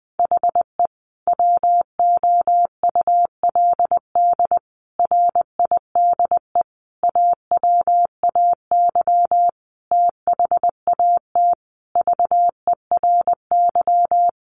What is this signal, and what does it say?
Morse code, heard as a keyed tone, about 20 wpm, 700 Hz. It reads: HE WOULD RIDE AWAY THAT VERY
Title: Sentences from Top 1200 Words 20wpm